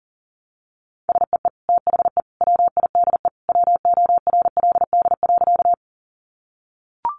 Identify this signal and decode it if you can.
Morse code, heard as a keyed tone, about 40 wpm, 700 Hz. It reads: SEE THE WIDE WORLD.
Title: Sentences from Top 900 Words 40wpm